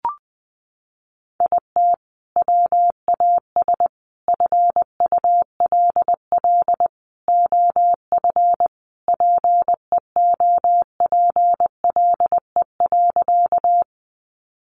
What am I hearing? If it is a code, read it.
Morse code, heard as a keyed tone, about 20 wpm, 700 Hz. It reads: IT WAS FULL OF PEOPLE.
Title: Sentences from Top 500 Words 20wpm